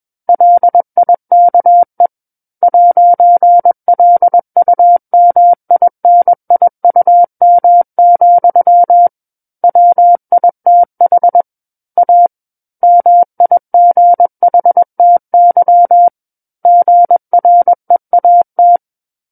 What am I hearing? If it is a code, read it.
Morse code, heard as a keyed tone, about 21 wpm, 700 Hz. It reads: LIKE 'LUMINIUM, WITH A MIGHTY GREAT
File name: Book One / 12-Chapter 12 - 21 WPM